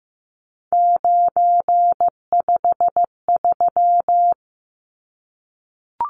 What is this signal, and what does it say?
Morse code, heard as a keyed tone, about 15 wpm, 700 Hz. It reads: 953
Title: Numbers - 3-Digit 15wpm